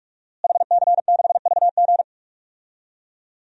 Morse code, heard as a keyed tone, about 46 wpm, 700 Hz. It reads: HC6VG
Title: DXCC-100 Call Signs - 5 Characters - Repeated 3x - 45wpm